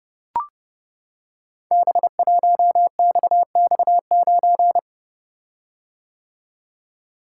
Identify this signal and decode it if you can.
Morse code, heard as a keyed tone, about 28 wpm, 700 Hz. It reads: B1XX9
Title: Code Groups - 5 Letter:Numbers - 30wpm